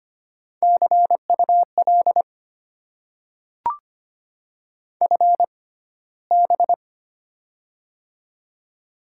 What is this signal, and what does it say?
Morse code, heard as a keyed tone, about 25 wpm, spaced as for 15 wpm, 700 Hz. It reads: CUL FB
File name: Top 90 QSO Elements - 6x Character Spacing - 25wpm